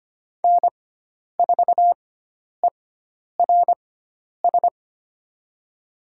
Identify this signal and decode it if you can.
Morse code, heard as a keyed tone, about 25 wpm, spaced as for 10 wpm, 700 Hz. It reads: N4ERS
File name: Lesson 16g - Letter c - Call Signs 25wpm